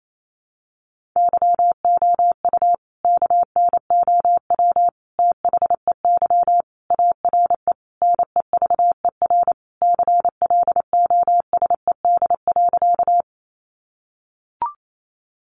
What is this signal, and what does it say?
Morse code, heard as a keyed tone, about 28 wpm, 700 Hz. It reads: YOU KNOW THEY ARE NEVER CLOSED.
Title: Sentences from Top 1400 Words 28wpm